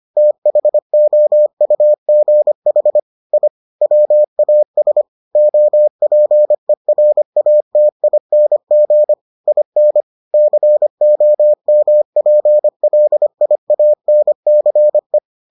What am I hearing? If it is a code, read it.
Morse code, heard as a keyed tone, about 25 wpm, 600 Hz. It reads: THOUGH I WAS OPERATING IN COMPLIANCE